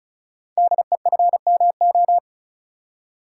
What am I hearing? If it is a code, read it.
Morse code, heard as a keyed tone, about 36 wpm, 700 Hz. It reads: DEFMO
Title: Code Groups - 5 Letters - 35wpm